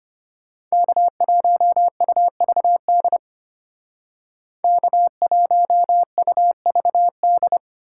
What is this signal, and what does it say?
Morse code, heard as a keyed tone, about 33 wpm, 700 Hz. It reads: K1UVD K1UVD